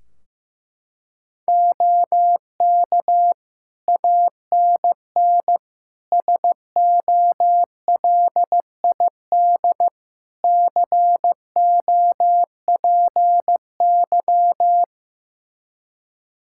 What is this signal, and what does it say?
Morse code, heard as a keyed tone, about 15 wpm, 700 Hz. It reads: OK ANN SOLID COPY
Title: K1USN SST QSO Practice 15wpm